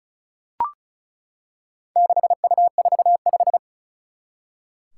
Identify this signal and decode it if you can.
Morse code, heard as a keyed tone, about 36 wpm, 700 Hz. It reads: 6U45
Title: Four Challenging Letter-Number Combinations 35wpm